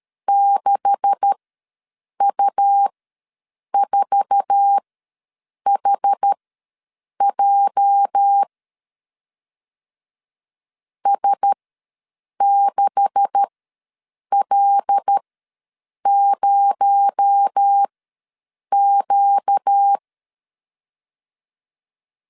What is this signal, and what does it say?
Morse code, heard as a keyed tone, about 13 wpm, spaced as for 7 wpm, 800 Hz. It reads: 6U4HJ S6L0Q
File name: LEZIONI / 9 286 ws 12wpm 3351